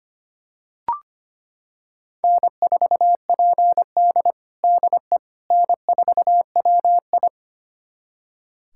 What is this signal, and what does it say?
Morse code, heard as a keyed tone, about 25 wpm, 700 Hz. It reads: N4PD DE N4WI
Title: USA Callsign DE Callsign - 4 Characters - 25wpm